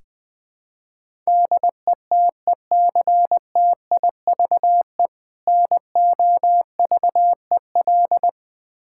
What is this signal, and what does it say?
Morse code, heard as a keyed tone, about 21 wpm, 700 Hz. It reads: DETECTIVE NOVEL
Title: Difficult Words with Hints - 20wpm